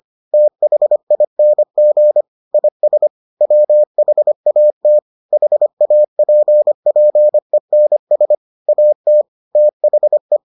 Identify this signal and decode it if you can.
Morse code, heard as a keyed tone, about 25 wpm, 600 Hz. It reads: THING IS WHAT HAPPENS AT THE